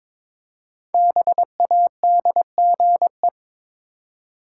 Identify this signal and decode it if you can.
Morse code, heard as a keyed tone, about 22 wpm, 690 Hz. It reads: BADGE
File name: 5 Letter Words - First 3 Letters Given - 22wpm